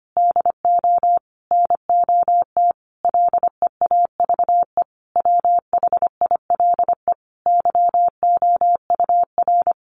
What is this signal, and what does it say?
Morse code, heard as a keyed tone, about 25 wpm, 700 Hz. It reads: DO NOT LEAVE WHILE YOUR
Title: Sentences from Top 1500 Words 25wpm